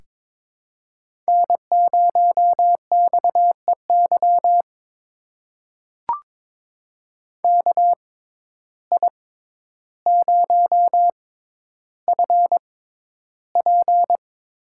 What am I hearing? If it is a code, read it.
Morse code, heard as a keyed tone, about 22 wpm, spaced as for 11 wpm, 700 Hz. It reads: N0XEY KI0FP
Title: USA Call Signs - 5 Characters - 6x Character Spacing - 22wpm